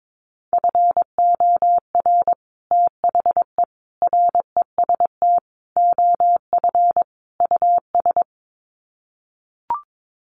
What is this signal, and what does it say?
Morse code, heard as a keyed tone, about 22 wpm, 700 Hz. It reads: FOR THE REST OF US
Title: Sets of 5 Words Encore 22wpm